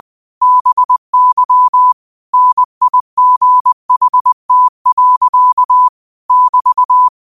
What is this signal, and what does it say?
Morse code, heard as a keyed tone, about 20 wpm, 1000 Hz. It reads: BY NIGHT. <BT>
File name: qotdfiles / 20wpm